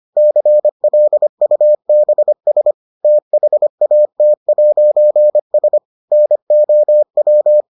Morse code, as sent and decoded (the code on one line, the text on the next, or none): -.-. .-.. ..- -... ... / - .... .- - .----. ... / -. --- .--
CLUBS THAT'S NOW